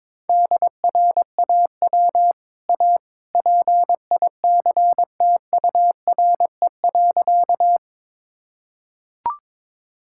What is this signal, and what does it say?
Morse code, heard as a keyed tone, about 22 wpm, 700 Hz. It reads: DRAW A PICTURE.
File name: practice / 22wpm / Easy Sentences with Hints - 22wpm